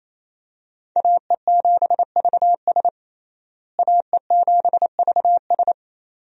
Morse code, heard as a keyed tone, about 28 wpm, 700 Hz. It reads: AE7VS AE7VS
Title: USA Call Signs - 5 Characters - Repeated 2x - 28wpm